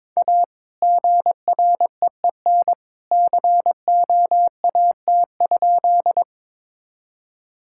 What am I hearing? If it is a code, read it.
Morse code, heard as a keyed tone, about 22 wpm, 700 Hz. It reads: A GREEN COAT?
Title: Sentences from Top 1600 Words 22wpm